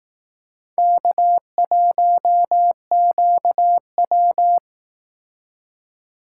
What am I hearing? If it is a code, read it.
Morse code, heard as a keyed tone, about 19 wpm, 700 Hz. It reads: K1QW
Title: Speed Racing - USA 1x2 Call Signs 15wpm